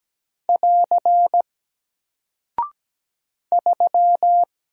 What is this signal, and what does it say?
Morse code, heard as a keyed tone, about 17 wpm, 700 Hz. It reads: <AR> 3
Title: Single Character (Letter:Number:Punctuation:Prosign) 17wpm